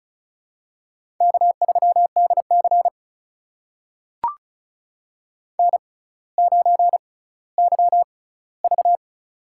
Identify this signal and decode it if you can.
Morse code, heard as a keyed tone, about 36 wpm, spaced as for 18 wpm, 700 Hz. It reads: K3DC N9YV
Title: USA Call Signs - 4 Characters - 6x Character Spacing - 35wpm